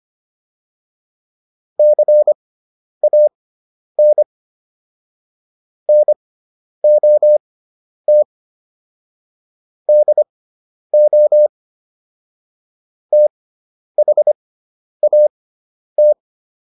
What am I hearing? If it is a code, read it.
Morse code, heard as a keyed tone, about 25 wpm, spaced as for 10 wpm, 600 Hz. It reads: CAN NOT DO THAT